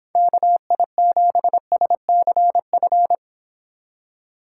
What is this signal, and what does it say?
Morse code, heard as a keyed tone, about 26 wpm, 700 Hz. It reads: KI7SCF